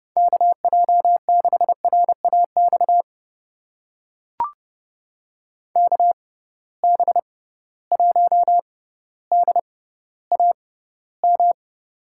Morse code, heard as a keyed tone, about 30 wpm, spaced as for 15 wpm, 700 Hz. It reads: KJ6RAX KB1DAM
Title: USA Call Signs - 6 Characters - 6x Character Spacing - 30wpm